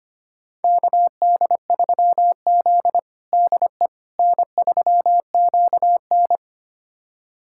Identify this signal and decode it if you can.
Morse code, heard as a keyed tone, about 25 wpm, 700 Hz. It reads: KD3Z DE N3QN
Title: USA Callsign DE Callsign - 4 Characters - 25wpm